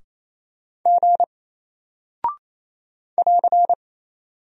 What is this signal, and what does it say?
Morse code, heard as a keyed tone, about 27 wpm, 700 Hz. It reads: G <AR>
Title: Single Character (Letter:Number:Punctuation:Prosign) Alt 28wpm